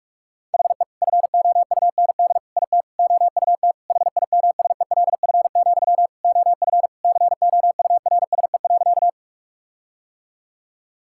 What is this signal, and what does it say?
Morse code, heard as a keyed tone, about 46 wpm, 700 Hz. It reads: HE FOUND IT OUT HIMSELF, OF COURSE.
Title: Sentences from Top 1600 Words 45wpm